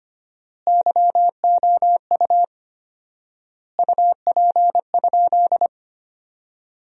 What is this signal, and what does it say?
Morse code, heard as a keyed tone, about 25 wpm, 700 Hz. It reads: YOU UP?
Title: 4x Word-Spacing - Sentences from Top 500 Words - 25wpm